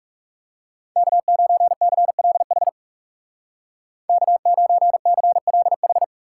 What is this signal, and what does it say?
Morse code, heard as a keyed tone, about 37 wpm, 700 Hz. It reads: K9CLH K9CLH
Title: Speed Racing - USA Call Signs 30wpm